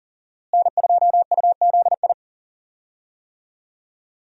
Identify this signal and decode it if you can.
Morse code, heard as a keyed tone, about 40 wpm, 700 Hz. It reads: N2UZI
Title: USA Call Signs - 5 Characters - Repeated 2x - 40wpm